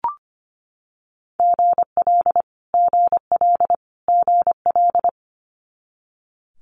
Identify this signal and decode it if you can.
Morse code, heard as a keyed tone, about 25 wpm, 700 Hz. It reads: GL GL GL
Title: Multi-letter Phonemes Repeated 3x 25wpm